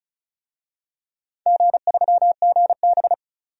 Morse code, heard as a keyed tone, about 36 wpm, 700 Hz. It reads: G3GB
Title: DXCC-100 Call Signs - 4 Characters - 35wpm